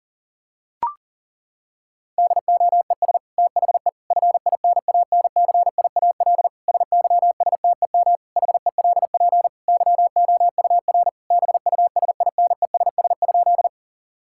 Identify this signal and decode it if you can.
Morse code, heard as a keyed tone, about 40 wpm, 700 Hz. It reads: DOES THE FINANCIAL SYSTEM HELP YOUR BUSINESS?
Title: Sentences from Top 1000 Words - Review Part 5 - 40wpm